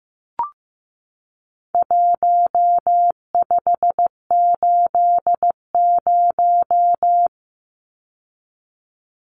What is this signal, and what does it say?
Morse code, heard as a keyed tone, about 15 wpm, 700 Hz. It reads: 1580
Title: Numbers - Contest Style - 4-Digit 15wpm